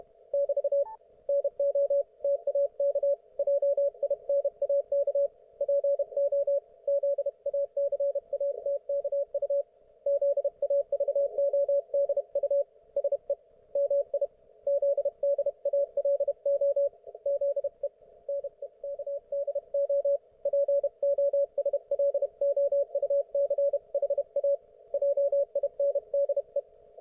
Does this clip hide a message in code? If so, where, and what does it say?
Yes: Morse code from 0.3 to 26.6 s, heard as a keyed tone, about 32 wpm, 560 Hz. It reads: <BT> NO TAK JINAK PO ZACATKU ZAVODU SE MI ZDALO ZE NEKDO POSLOUCHA JINDE